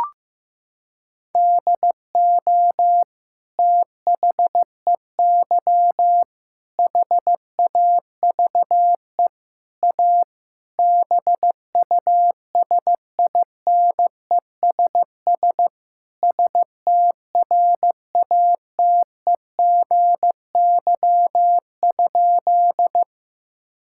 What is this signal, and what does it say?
Morse code, heard as a keyed tone, about 15 wpm, 700 Hz. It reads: DO THEY HAVE A BUSINESS STRATEGY?
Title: Sentences from Top 1000 Words - Review Part 5 - 15wpm